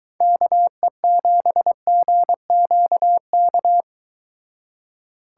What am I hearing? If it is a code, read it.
Morse code, heard as a keyed tone, about 24 wpm, 700 Hz. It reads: KE7GQK